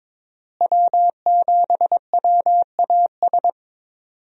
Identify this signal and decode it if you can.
Morse code, heard as a keyed tone, about 22 wpm, 700 Hz. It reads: W7WAS
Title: USA Call Signs - 5 Characters - Repeated 3x - 22wpm